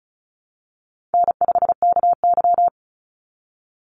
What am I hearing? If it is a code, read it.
Morse code, heard as a keyed tone, about 34 wpm, 700 Hz. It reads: N5KY
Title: Easy Call Signs Repeated 3x - 35wpm